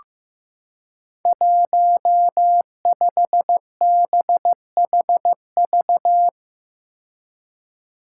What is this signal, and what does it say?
Morse code, heard as a keyed tone, about 16 wpm, 700 Hz. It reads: 15BHV
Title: Five Challenging Letter-Number Combinations 15wpm